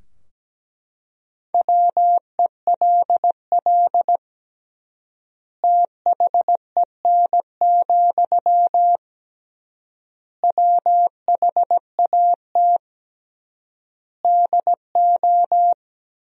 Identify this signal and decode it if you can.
Morse code, heard as a keyed tone, about 17 wpm, 700 Hz. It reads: WELL THEN, WHAT DO
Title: 3x Word-Spacing - Sentences from Top 200 Words - 17wpm